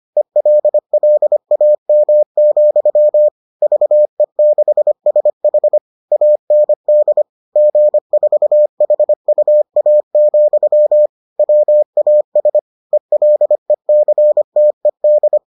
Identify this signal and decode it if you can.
Morse code, heard as a keyed tone, about 25 wpm, 600 Hz. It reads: ELLAM, VE6SH AND G4HUA, WAS ELECTED